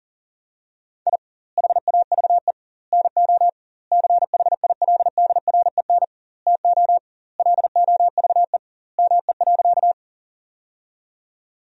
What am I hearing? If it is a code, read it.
Morse code, heard as a keyed tone, about 40 wpm, 700 Hz. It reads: I HAVE NO CHILDREN TO LOVE ME.
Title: 2x Word-Spacing - Sentences from Top 400 Words - 40wpm